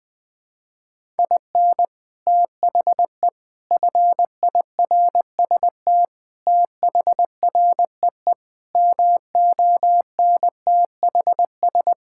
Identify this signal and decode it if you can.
Morse code, heard as a keyed tone, about 20 wpm, 700 Hz. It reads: IN THE FIRST THREE MONTHS
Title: Sentences with 6-Word Phrases - 20wpm